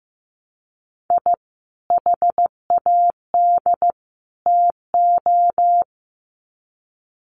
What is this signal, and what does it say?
Morse code, heard as a keyed tone, about 15 wpm, 700 Hz. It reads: I HAD TO
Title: Sets of 3 Words 15wpm